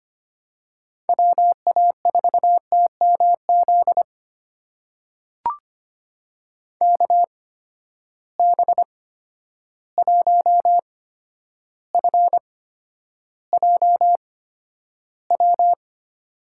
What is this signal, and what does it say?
Morse code, heard as a keyed tone, about 25 wpm, spaced as for 11 wpm, 700 Hz. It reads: WA4TMZ KB1FJW
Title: USA Call Signs - 6 Characters - 8x Character Spacing - 25wpm